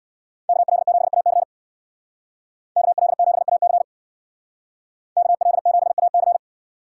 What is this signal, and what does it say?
Morse code, heard as a keyed tone, about 64 wpm, 700 Hz. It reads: KL7AC KL7AC KL7AC